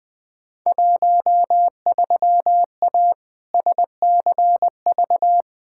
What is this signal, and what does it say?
Morse code, heard as a keyed tone, about 20 wpm, 700 Hz. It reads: 13A SCV
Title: Field Day Exchanges Encore - 20wpm